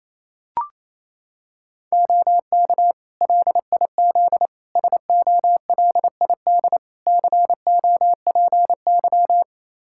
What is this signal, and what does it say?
Morse code, heard as a keyed tone, about 28 wpm, 700 Hz. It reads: OK LIZ SOLID COPY
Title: K1USN SST QSO Practice 28wpm